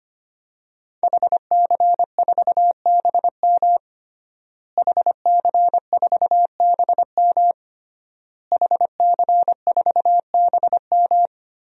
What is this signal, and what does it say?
Morse code, heard as a keyed tone, about 25 wpm, 700 Hz. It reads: HC4BM HC4BM HC4BM
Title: DXCC-100 Call Signs - 5 Characters - Repeated 3x - 25wpm